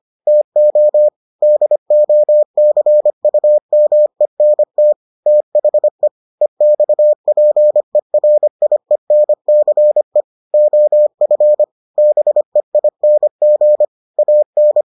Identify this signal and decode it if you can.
Morse code, heard as a keyed tone, about 25 wpm, 600 Hz. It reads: TO DOCUMENT THE EXPERIENCE OF BEING AN